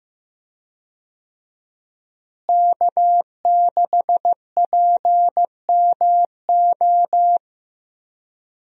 Morse code, heard as a keyed tone, about 15 wpm, 700 Hz. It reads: K6PMO
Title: USA Call Signs - 5 Characters - Repeated 2x - 15wpm